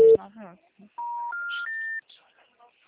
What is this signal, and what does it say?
Special information tone, with speech.